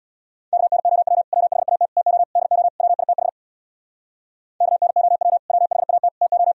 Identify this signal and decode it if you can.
Morse code, heard as a keyed tone, about 77 wpm, 700 Hz. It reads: KNOW WHAT TO DO WITH KNOW WHAT TO